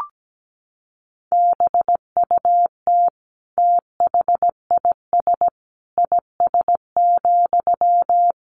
Morse code, heard as a keyed tone, about 17 wpm, 700 Hz. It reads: BUT THIS IS,
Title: Sentences from Top 1500 Words 17wpm